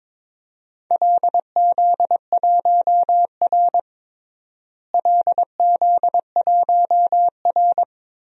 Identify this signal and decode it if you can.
Morse code, heard as a keyed tone, about 22 wpm, 700 Hz. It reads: LZ1R LZ1R
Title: DXCC-100 Call Signs - 4 Characters - Repeated 2x - 22wpm